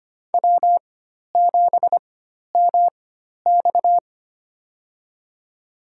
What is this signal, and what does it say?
Morse code, heard as a keyed tone, about 25 wpm, spaced as for 12 wpm, 700 Hz. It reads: W7MX
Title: USA Call Signs - 4 Characters - 4x Character Spacing - 25wpm